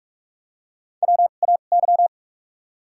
Morse code, heard as a keyed tone, about 44 wpm, spaced as for 32 wpm, 700 Hz. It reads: WAY
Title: 2x Character Spacing - Top 100 Words - 45wpm